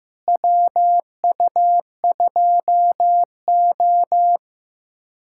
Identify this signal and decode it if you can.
Morse code, heard as a keyed tone, about 16 wpm, 700 Hz. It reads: WU2O
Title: USA Callsign DE Callsign - 4 Characters - 15wpm